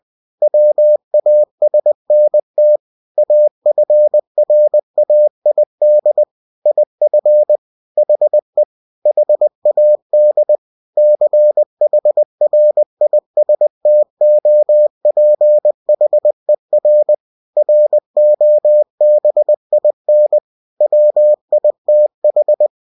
Morse code, heard as a keyed tone, about 20 wpm, 600 Hz. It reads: WASNT AFRAID IF HE HAD CHRISTOPHER ROBIN WITH
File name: cw-books / winnie-the-pooh / winnie-20wpm-page-0043